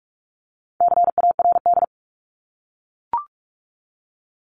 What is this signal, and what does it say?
Morse code, heard as a keyed tone, about 44 wpm, 700 Hz. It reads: CARD
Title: Top 501-750 Words 45wpm